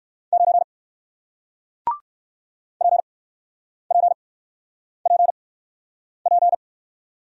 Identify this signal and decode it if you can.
Morse code, heard as a keyed tone, about 55 wpm, spaced as for 9 wpm, 700 Hz. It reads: 9 PPPP
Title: Speed-Racing Single Letter-Number 45wpm